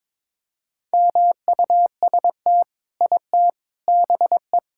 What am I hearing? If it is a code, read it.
Morse code, heard as a keyed tone, about 22 wpm, 700 Hz. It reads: MUST IT BE
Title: Sentences from Top 1000 Words - Review Part 5 - 22wpm